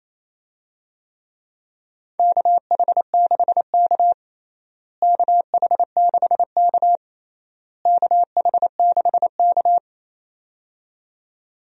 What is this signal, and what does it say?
Morse code, heard as a keyed tone, about 27 wpm, 700 Hz. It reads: KH6K KH6K KH6K
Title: DXCC-100 Call Signs - 4 Characters - Repeated 3x - 28wpm